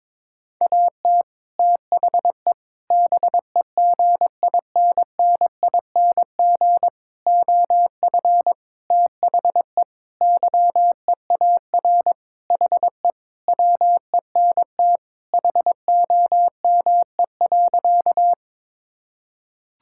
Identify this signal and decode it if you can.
Morse code, heard as a keyed tone, about 22 wpm, 700 Hz. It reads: AT THE BEGINNING OF THE YEAR HE WENT HOME.
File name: practice / 22wpm / Sentences with 4-Word Phrases - 22wpm